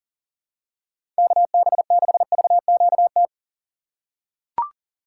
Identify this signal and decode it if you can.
Morse code, heard as a keyed tone, about 37 wpm, 700 Hz. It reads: KB6VQT
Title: USA Call Signs - 6 Characters - 2x Character Spacing - 40wpm